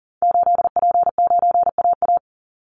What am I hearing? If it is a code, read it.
Morse code, heard as a keyed tone, about 40 wpm, 700 Hz. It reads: 8P9AA
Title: Call Signs - 40wpm